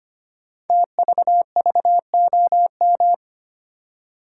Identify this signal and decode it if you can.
Morse code, heard as a keyed tone, about 25 wpm, 700 Hz. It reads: TVVOM
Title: Code Groups - 5 Letters - 25wpm